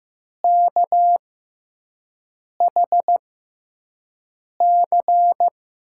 Morse code, heard as a keyed tone, about 15 wpm, spaced as for 5 wpm, 700 Hz. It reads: KHC
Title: USA Call Signs - 5 Characters - 6x Character Spacing - 15wpm